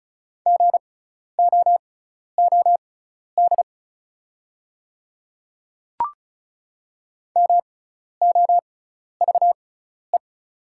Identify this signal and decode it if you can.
Morse code, heard as a keyed tone, about 36 wpm, spaced as for 12 wpm, 700 Hz. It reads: GOOD MOVE
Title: 6x Character Spacing - Top 300 Words - 35wpm